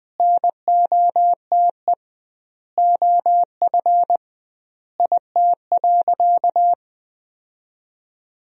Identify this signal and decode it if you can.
Morse code, heard as a keyed tone, about 21 wpm, 700 Hz. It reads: NOTE OF IT.
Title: 2x Word-Spacing - Sentences from Top 500 Words - 20wpm